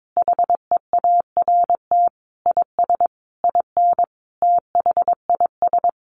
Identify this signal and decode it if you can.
Morse code, heard as a keyed tone, about 22 wpm, 700 Hz. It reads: HEART IS IN THIS
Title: Sentences from Top 1500 Words 22wpm